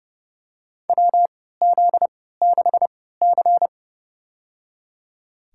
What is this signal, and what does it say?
Morse code, heard as a keyed tone, about 30 wpm, spaced as for 17 wpm, 700 Hz. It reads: WZ6C
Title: USA Call Signs - 4 Characters - 3x Character Spacing - 30wpm